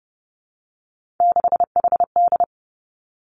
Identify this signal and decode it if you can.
Morse code, heard as a keyed tone, about 30 wpm, 700 Hz. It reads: 6HD
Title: Three Challenging Letter-Number Combinations 30wpm